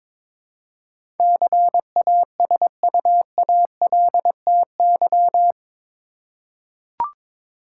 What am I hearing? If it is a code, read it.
Morse code, heard as a keyed tone, about 22 wpm, 700 Hz. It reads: CASUALTY
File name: practice / 22wpm / Eight Letter Words - 22wpm